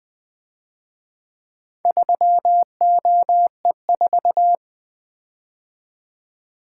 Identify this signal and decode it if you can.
Morse code, heard as a keyed tone, about 24 wpm, 700 Hz. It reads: 3OE4
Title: Speed-Racing - Four Letter-Number Combinations - 20wpm